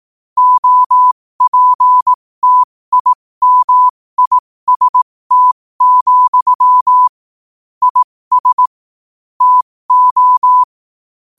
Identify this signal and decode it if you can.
Morse code, heard as a keyed tone, about 18 wpm, spaced as for 15 wpm, 1000 Hz. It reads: OPTIMIST, IS TO
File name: qotdfiles / 15wpm